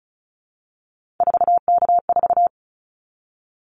Morse code, heard as a keyed tone, about 36 wpm, 700 Hz. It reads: 4K4
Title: Callsign Prefixes for Top 100 DXCC - 35wpm